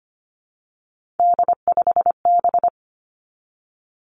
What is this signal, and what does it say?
Morse code, heard as a keyed tone, about 25 wpm, 700 Hz. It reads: D5B
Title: Three Challenging Letter-Number Combinations 25wpm